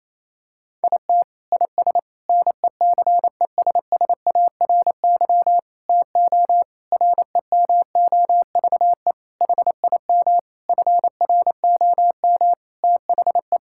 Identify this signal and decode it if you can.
Morse code, heard as a keyed tone, about 28 wpm, 700 Hz. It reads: IT IS NECESSARY TO REMOVE HIM FROM THE
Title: Sentences from Top 1000 Words - Review Part 5 - 28wpm